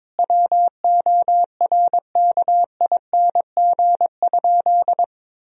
Morse code, heard as a keyed tone, about 22 wpm, 700 Hz. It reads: WORKING?
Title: Sentences from Top 1000 Words - Review Part 4 - 22wpm